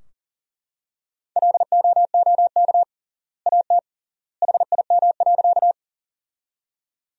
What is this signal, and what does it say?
Morse code, heard as a keyed tone, about 37 wpm, 700 Hz. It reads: LOOK AT HIM.
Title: 3x Word-Spacing - Sentences from Top 100 Words - 40wpm